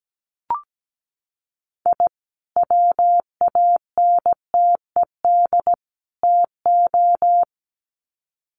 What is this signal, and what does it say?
Morse code, heard as a keyed tone, about 17 wpm, 700 Hz. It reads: I WANTED TO
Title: Sets of 3 Words 17wpm